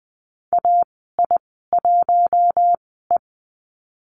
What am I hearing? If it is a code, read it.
Morse code, heard as a keyed tone, about 20 wpm, spaced as for 14 wpm, 700 Hz. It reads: AI1E
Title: 082 Lesson 14g - Number 2 - Call Signs 20wpm